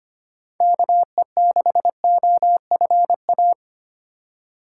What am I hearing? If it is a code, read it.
Morse code, heard as a keyed tone, about 25 wpm, 700 Hz. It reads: KE6OFA
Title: USA Call Signs - 3x Character Spacing - 25wpm